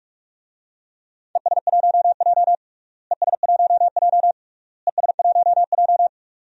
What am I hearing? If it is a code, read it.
Morse code, heard as a keyed tone, about 47 wpm, 700 Hz. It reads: ES1J ES1J ES1J
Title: DXCC-100 Call Signs - 4 Characters - Repeated 3x - 45wpm